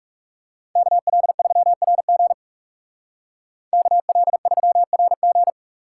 Morse code, heard as a keyed tone, about 39 wpm, 700 Hz. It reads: KL3RG KL3RG